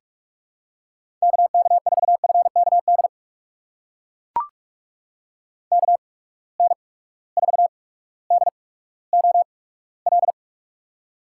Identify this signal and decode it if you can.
Morse code, heard as a keyed tone, about 43 wpm, spaced as for 19 wpm, 700 Hz. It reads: KK4FKD KN4DOL
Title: USA Call Signs - 6 Characters - 8x Character Spacing - 45wpm